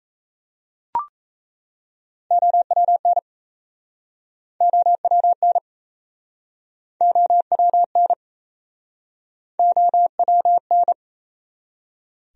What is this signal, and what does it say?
Morse code, heard as a keyed tone, about 40 wpm, 700 Hz. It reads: OWN OWN OWN OWN